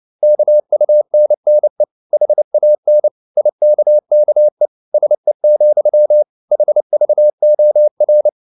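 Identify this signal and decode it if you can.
Morse code, heard as a keyed tone, about 29 wpm, 600 Hz. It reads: KUNNE HAN IKKE SE, HVOR
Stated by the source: Fast (QRQ) Morse at 29 wpm, from CW med Gnister.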